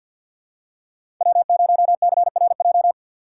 Morse code, heard as a keyed tone, about 52 wpm, 690 Hz. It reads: W0CRJ
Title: USA Call Signs - 5 Characters - 6x Character Spacing - 50wpm